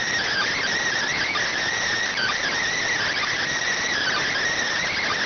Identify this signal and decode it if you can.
Emergency Alert System (SAME) tones, heard as harsh buzzing bursts. It is heard from 0.2 to 4.3 s.